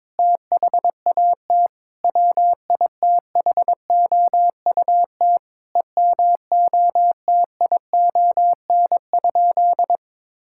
Morse code, heard as a keyed tone, about 22 wpm, 700 Hz. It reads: THAT WITHOUT EMOTION?
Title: Sentences from Top 1600 Words 22wpm